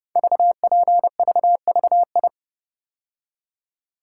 Morse code, heard as a keyed tone, about 30 wpm, 700 Hz. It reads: VPVVI
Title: Code Groups - 5 Letters - 30wpm